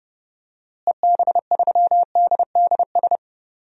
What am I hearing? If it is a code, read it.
Morse code, heard as a keyed tone, about 31 wpm, 700 Hz. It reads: EB3DDS
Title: DXCC-100 Call Signs - 6 Characters - Repeated 2x - 30wpm